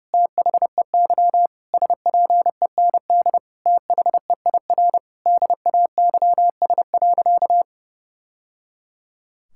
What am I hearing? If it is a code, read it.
Morse code, heard as a keyed tone, about 30 wpm, 700 Hz. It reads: THEY SPEND THEIR DAYS.
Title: Sentences from Top 1000 Words - Review Part 5 - 30wpm